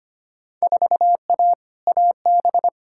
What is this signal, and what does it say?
Morse code, heard as a keyed tone, about 25 wpm, 700 Hz. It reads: 4A AB
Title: Field Day Exchanges Encore - 25wpm